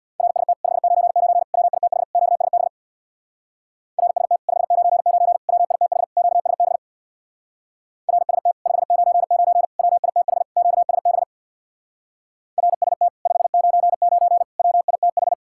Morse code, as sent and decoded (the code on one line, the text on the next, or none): .-. ... - / ..... ----. ----. / .-- .. - .... / --.- ... -... / .-. ... - / ..... ----. ----. / .-- .. - .... / --.- ... -... / .-. ... - / ..... ----. ----. / .-- .. - .... / --.- ... -... / .-. ... - / ..... ----. ----. / .-- .. - ....
RST 599 WITH QSB RST 599 WITH QSB RST 599 WITH QSB RST 599 WITH